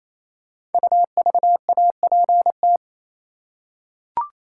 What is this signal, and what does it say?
Morse code, heard as a keyed tone, about 29 wpm, 700 Hz. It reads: UVAPT
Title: Code Groups - 5 Letter:Numbers - 28wpm